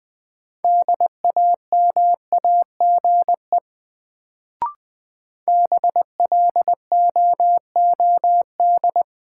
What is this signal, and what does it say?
Morse code, heard as a keyed tone, about 20 wpm, 700 Hz. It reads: DAMAGE BLOOD
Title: Sentences from Top 1100 Words - New Word Drill Alt - 20wpm